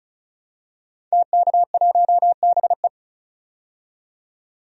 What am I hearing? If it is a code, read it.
Morse code, heard as a keyed tone, about 34 wpm, 700 Hz. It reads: TK1BE
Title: DXCC-100 Call Signs - 5 Characters - 35wpm